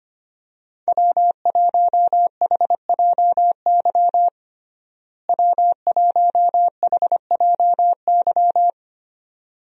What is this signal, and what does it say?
Morse code, heard as a keyed tone, about 25 wpm, 700 Hz. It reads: W1HJY W1HJY
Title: USA Call Signs - 5 Characters - Repeated 2x - 25wpm